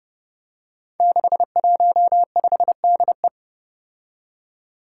Morse code, heard as a keyed tone, about 30 wpm, 700 Hz. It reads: 615DE
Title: Five Challenging Letter-Number Combinations 30wpm